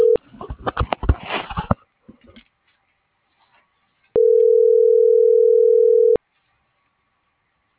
Ringback tone (North America), 440 + 480 Hz.